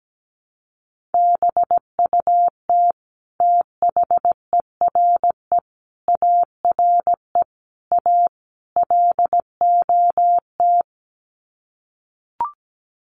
Morse code, heard as a keyed tone, about 17 wpm, 700 Hz. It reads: BUT THERE ARE A LOT
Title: Sets of 5 Words Encore2 17wpm